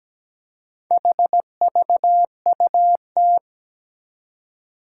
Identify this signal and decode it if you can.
Morse code, heard as a keyed tone, about 17 wpm, 700 Hz. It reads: HVUT
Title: Four Challenging Letter-Number Combinations 17wpm